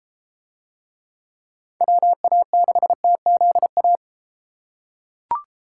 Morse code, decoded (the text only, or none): WA6TZA